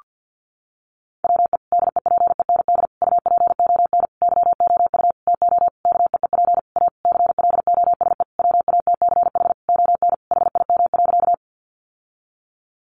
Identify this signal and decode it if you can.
Morse code, heard as a keyed tone, about 50 wpm, 700 Hz. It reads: WE DEPEND UPON YOU TO KEEP A CLOSE WATCH ON HIM.
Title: Sentences from Top 1300 Words 50wpm